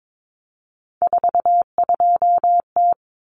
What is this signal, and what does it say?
Morse code, heard as a keyed tone, about 22 wpm, 700 Hz. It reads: 42T
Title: Common Cut Numbers - 3-Digit 22wpm